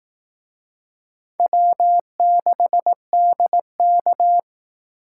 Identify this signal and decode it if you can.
Morse code, heard as a keyed tone, about 18 wpm, 700 Hz. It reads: W6DK
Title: Speed Racing - USA 1x2 Call Signs 15wpm